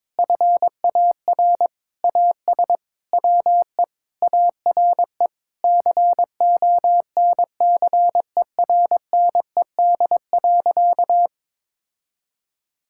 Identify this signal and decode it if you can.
Morse code, heard as a keyed tone, about 22 wpm, 700 Hz. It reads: FAR AS WE ARE CONCERNED.
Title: Sentences from Top 1000 Words - Review Part 6 - 22wpm